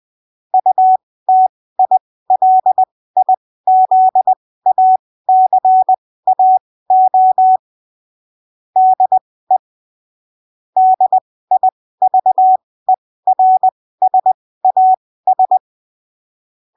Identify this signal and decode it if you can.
Morse code, heard as a keyed tone, about 20 wpm, spaced as for 15 wpm, 750 Hz. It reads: UTILIZACAO DE DIVERSAS
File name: arquivos / texto / texto 06 20ppm